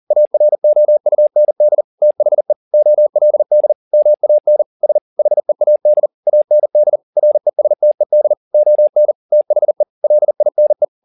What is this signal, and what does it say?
Morse code, heard as a keyed tone, about 40 wpm, 600 Hz. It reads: AROUND THE OLD MAN S HEAD AND RESTED ON THE LINE